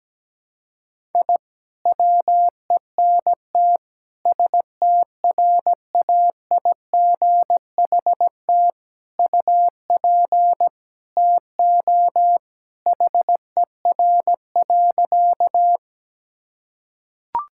Morse code, heard as a keyed tone, about 17 wpm, 700 Hz. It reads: I WENT STRAIGHT UP TO HER.
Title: Sentences from Top 1000 Words - Review Part 6 - 17wpm